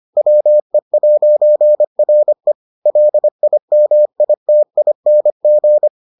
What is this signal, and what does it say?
Morse code, heard as a keyed tone, about 25 wpm, 600 Hz. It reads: WE'RE LIMITING